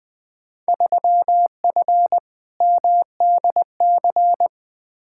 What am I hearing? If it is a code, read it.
Morse code, heard as a keyed tone, about 20 wpm, 700 Hz. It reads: 3F MDC
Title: Field Day Exchanges Encore - 20wpm